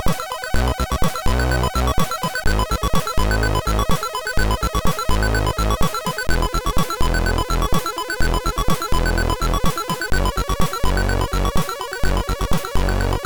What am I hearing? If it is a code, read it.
There is no code.